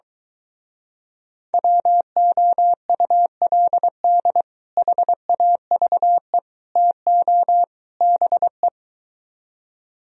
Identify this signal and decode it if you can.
Morse code, heard as a keyed tone, about 24 wpm, 700 Hz. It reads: WOULD HAVE TO BE